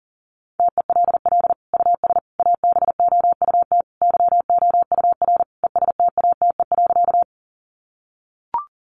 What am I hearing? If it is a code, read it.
Morse code, heard as a keyed tone, about 40 wpm, 700 Hz. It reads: TELL US ABOUT YOUR ESTATE.
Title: Sentences from Top 1200 Words 40wpm